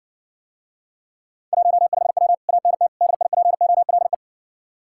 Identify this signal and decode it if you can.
Morse code, heard as a keyed tone, about 60 wpm, 700 Hz. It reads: 15W ANT DIPOLE